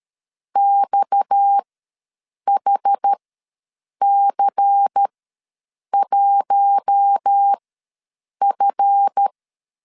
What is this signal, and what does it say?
Morse code, heard as a keyed tone, about 13 wpm, spaced as for 7 wpm, 800 Hz. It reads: XHC1F
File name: LEZIONI / 7 GF9Z / 7 GF ws 12wpm 3140